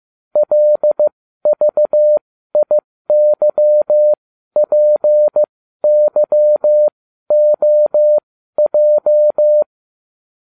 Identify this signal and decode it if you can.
Morse code, heard as a keyed tone, about 15 wpm, spaced as for 12 wpm, 600 Hz. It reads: LVIYPYOJ